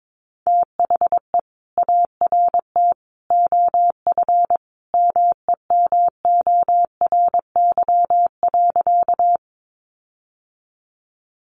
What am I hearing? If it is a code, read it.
Morse code, heard as a keyed tone, about 22 wpm, 700 Hz. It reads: THE ART OF MEMORY.
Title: Sentences from Top 1300 Words 22wpm